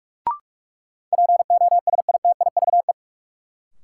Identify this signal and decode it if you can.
Morse code, heard as a keyed tone, about 41 wpm, 700 Hz. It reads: POSITIVE
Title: Sentences from Top 1000 Words - Review Part 5 - New Word Drill - 45wpm